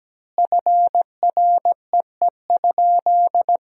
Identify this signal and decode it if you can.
Morse code, heard as a keyed tone, about 17 wpm, 700 Hz. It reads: FREE?
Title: 4x Word-Spacing - Sentences from Top 500 Words - 17wpm